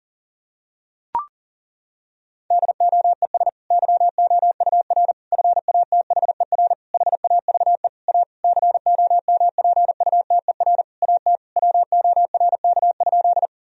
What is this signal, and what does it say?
Morse code, heard as a keyed tone, about 40 wpm, 700 Hz. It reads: DOES YOUR FATHER HAVE A COMPUTER AT WORK?
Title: Sentences from Top 1000 Words - Review Part 3 - 40wpm